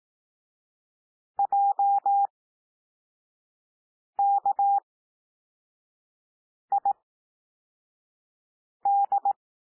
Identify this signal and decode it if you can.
Morse code, heard as a keyed tone, about 18 wpm, spaced as for 4 wpm, 800 Hz. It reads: JKID
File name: cwmp3 / L2 SOUND LETTERS 18 wpm